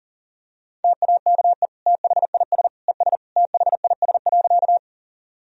Morse code, heard as a keyed tone, about 40 wpm, 700 Hz. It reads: TAKE THIS ES THIS.
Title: Sentences from Top 200 Words and Abbreviations 40wpm